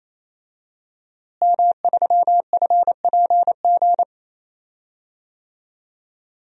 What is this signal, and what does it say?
Morse code, heard as a keyed tone, about 28 wpm, 700 Hz. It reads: M3FPG
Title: DXCC-100 Call Signs - 5 Characters - Repeated 3x - 28wpm